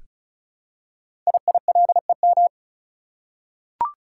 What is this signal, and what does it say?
Morse code, heard as a keyed tone, about 36 wpm, 700 Hz. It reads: IILEM
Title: Code Groups - 5 Letter:Numbers - 35wpm